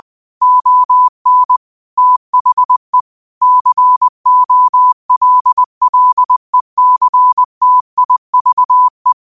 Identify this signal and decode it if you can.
Morse code, heard as a keyed tone, about 20 wpm, 1000 Hz. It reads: ON THE COLLECTIVE